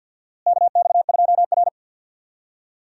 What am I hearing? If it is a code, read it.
Morse code, heard as a keyed tone, about 51 wpm, 700 Hz. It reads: KX2R
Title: USA Call Signs - 4 Characters - Repeated 3x - 50wpm